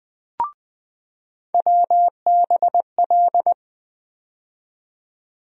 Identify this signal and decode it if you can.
Morse code, heard as a keyed tone, about 20 wpm, 700 Hz. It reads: WBL
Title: USA Call Sign Postfixes - 3 Letters 20wpm